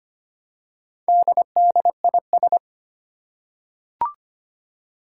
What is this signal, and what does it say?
Morse code, heard as a keyed tone, about 25 wpm, 700 Hz. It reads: DDIS
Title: Four Challenging Letter-Number Combinations 25wpm